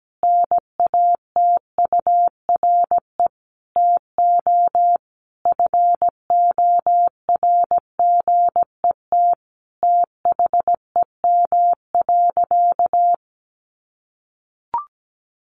Morse code, heard as a keyed tone, about 17 wpm, 700 Hz. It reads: NATURE TO FORGET THEM.
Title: Sentences from Top 1200 Words 17wpm